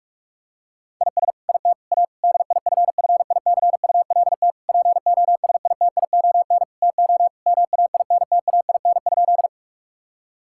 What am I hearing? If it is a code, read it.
Morse code, heard as a keyed tone, about 42 wpm, 700 Hz. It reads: IS IT A DIFFICULT POSITION TO MAINTAIN?
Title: Sentences from Top 1000 Words - Review Part 5 - 45wpm